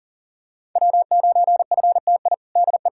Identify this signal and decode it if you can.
Morse code, heard as a keyed tone, about 40 wpm, 700 Hz. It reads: W9FTI DE
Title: K1USN SST QSO Practice 40wpm